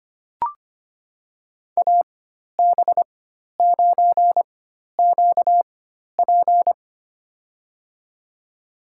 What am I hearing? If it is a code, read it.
Morse code, heard as a keyed tone, about 25 wpm, spaced as for 12 wpm, 700 Hz. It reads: AB9QP
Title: USA Call Signs - 5 Characters - 4x Character Spacing - 25wpm